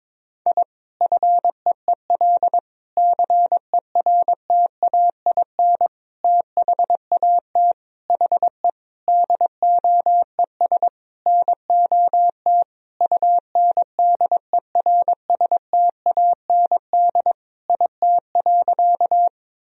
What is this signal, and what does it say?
Morse code, heard as a keyed tone, about 22 wpm, 700 Hz. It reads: I FEEL CERTAIN THAT HE DOES NOT UNDERSTAND IT.
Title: Sentences from Top 1000 Words - Review Part 2 - 22wpm